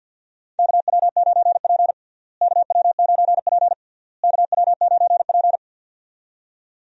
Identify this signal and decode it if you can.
Morse code, heard as a keyed tone, about 51 wpm, 700 Hz. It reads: KW9P KW9P KW9P
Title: USA Call Signs - 4 Characters - Repeated 3x - 50wpm